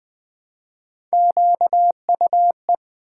Morse code, heard as a keyed tone, about 20 wpm, 700 Hz. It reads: QUE
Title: Build A Word - 1 Letter At A Time - 5-Letter Words - 20wpm